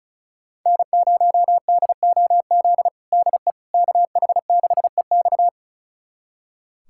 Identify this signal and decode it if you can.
Morse code, heard as a keyed tone, about 36 wpm, 690 Hz. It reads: N0DOZ DE KH6EX
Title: USA Callsign DE Callsign - 5 Characters - 35wpm